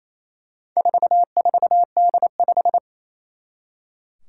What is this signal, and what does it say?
Morse code, heard as a keyed tone, about 27 wpm, 700 Hz. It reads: 44D5
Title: Four Challenging Letter-Number Combinations 28wpm